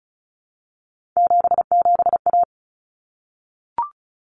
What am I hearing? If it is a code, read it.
Morse code, heard as a keyed tone, about 34 wpm, 700 Hz. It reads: 77A
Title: Common Cut Numbers - 3-Digit 35wpm